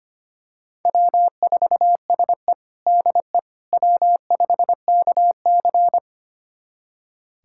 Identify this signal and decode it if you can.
Morse code, heard as a keyed tone, about 25 wpm, 700 Hz. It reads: W4SE DE W5KC
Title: USA Callsign DE Callsign - 4 Characters - 25wpm